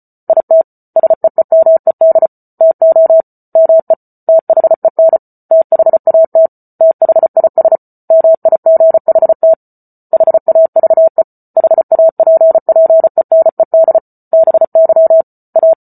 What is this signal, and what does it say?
Morse code, heard as a keyed tone, about 34 wpm, 650 Hz. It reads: IT SEEMED TO ME THEN THAT THIS MIGHT HAVE HAPPENED BY A